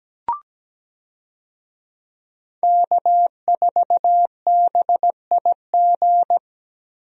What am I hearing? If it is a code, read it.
Morse code, heard as a keyed tone, about 17 wpm, 700 Hz. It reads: K4BIG
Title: USA Call Signs - 5 Characters - 17wpm